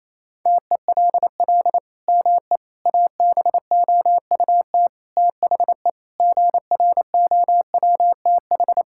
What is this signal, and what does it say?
Morse code, heard as a keyed tone, about 28 wpm, 700 Hz. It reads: TELL ME ABOUT THE GROWTH
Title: Sentences from Top 1000 Words - Review Part 5 - 28wpm